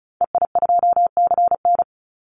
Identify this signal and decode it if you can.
Morse code, heard as a keyed tone, about 36 wpm, 700 Hz. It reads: EI2CN
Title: Call Signs - 35wpm